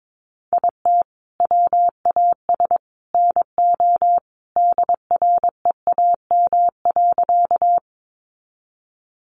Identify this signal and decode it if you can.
Morse code, heard as a keyed tone, about 22 wpm, 700 Hz. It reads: IT WAS NO DREAM.
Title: Sentences from Top 1500 Words 22wpm